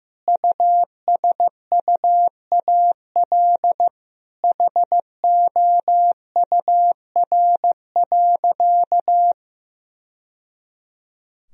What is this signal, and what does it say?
Morse code, heard as a keyed tone, about 15 wpm, 700 Hz. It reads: USUAL HOUR.
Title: Sentences from Top 1000 Words - Review Part 2 - 15wpm